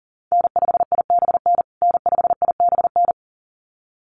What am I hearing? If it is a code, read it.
Morse code, heard as a keyed tone, about 40 wpm, 700 Hz. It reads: N5IBN N5IBN
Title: USA Call Signs Repeated 2x 40wpm